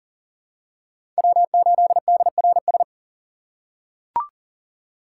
Morse code, heard as a keyed tone, about 40 wpm, 700 Hz. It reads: W8DRS
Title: USA Call Signs - 4x Character Spacing - 40wpm